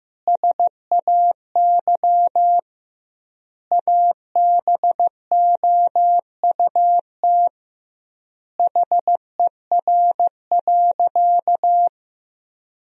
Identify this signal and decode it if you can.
Morse code, heard as a keyed tone, about 15 wpm, 700 Hz. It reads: SAY ABOUT HER.
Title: 2x Word-Spacing - Sentences from Top 200 Words - 15wpm